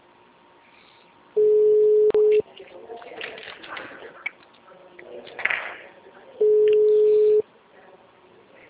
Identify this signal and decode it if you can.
Ringback tone (Europe, 425 Hz).
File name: out- -2003- - - .16767